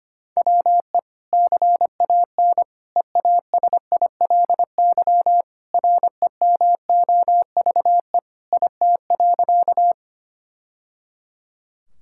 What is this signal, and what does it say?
Morse code, heard as a keyed tone, about 25 wpm, 700 Hz. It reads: WE CAN EASILY REMOVE IT.
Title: Sentences from Top 1000 Words - Review Part 5 - 25wpm